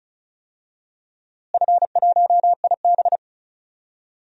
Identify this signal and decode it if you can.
Morse code, heard as a keyed tone, about 36 wpm, 700 Hz. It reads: F1IB
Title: DXCC-100 Call Signs - 4 Characters - Repeated 3x - 35wpm